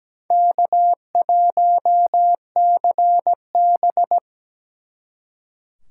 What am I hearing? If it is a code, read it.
Morse code, heard as a keyed tone, about 20 wpm, 700 Hz. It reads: K1CB
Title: Speed Racing - USA 1x2 Call Signs 17wpm